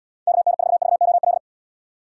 Morse code, heard as a keyed tone, about 74 wpm, 700 Hz. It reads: KN4VOL